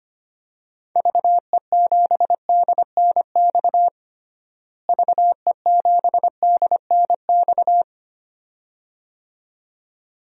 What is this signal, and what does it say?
Morse code, heard as a keyed tone, about 25 wpm, 700 Hz. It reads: VE7DNX VE7DNX
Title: DXCC-100 Call Signs - 6 Characters - Repeated 2x - 25wpm